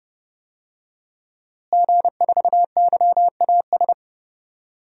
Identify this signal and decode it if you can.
Morse code, heard as a keyed tone, about 30 wpm, 700 Hz. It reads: G4YAS
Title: DXCC-100 Call Signs - 5 Characters - Repeated 2x - 30wpm